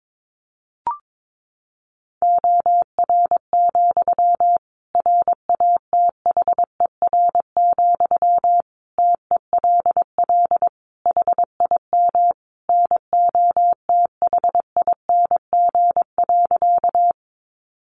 Morse code, heard as a keyed tone, about 22 wpm, 700 Hz. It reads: OR, RATHER, TELL HIM NOTHING.
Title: Sentences from Top 700 Words 22wpm